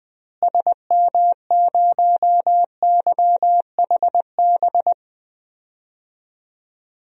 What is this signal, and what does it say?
Morse code, heard as a keyed tone, about 20 wpm, 700 Hz. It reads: SM0YHB
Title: DXCC-100 Call Signs - 6 Characters - Repeated 2x - 20wpm